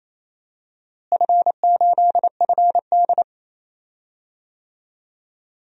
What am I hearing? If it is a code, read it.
Morse code, heard as a keyed tone, about 28 wpm, 700 Hz. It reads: F8FD
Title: DXCC-100 Call Signs - 4 Characters - Repeated 3x - 28wpm